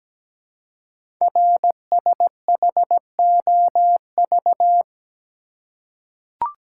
Morse code, heard as a keyed tone, about 17 wpm, 700 Hz. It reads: RSHOV
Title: Code Groups - 5 Letter:Numbers - 17wpm